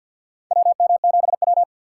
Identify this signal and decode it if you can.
Morse code, heard as a keyed tone, about 53 wpm, 700 Hz. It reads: WM7W
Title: USA Call Signs - 4 Characters - 3x Character Spacing - 50wpm